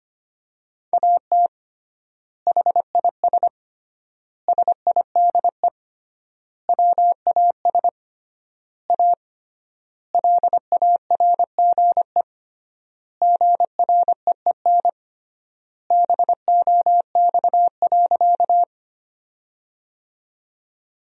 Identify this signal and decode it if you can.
Morse code, heard as a keyed tone, about 25 wpm, 700 Hz. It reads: AT HIS SIDE WAS A LARGE GREEN BOX.
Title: 3x Word-Spacing - Sentences from Top 500 Words - 25wpm